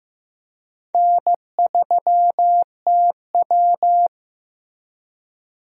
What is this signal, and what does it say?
Morse code, heard as a keyed tone, about 15 wpm, 700 Hz. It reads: N3TW
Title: USA Call Signs - 4 Characters - Repeated 3x - 15wpm